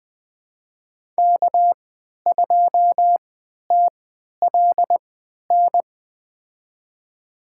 Morse code, heard as a keyed tone, about 20 wpm, spaced as for 11 wpm, 700 Hz. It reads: K2TLN
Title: USA Call Signs - 5 Characters - 3x Character Spacing - 20wpm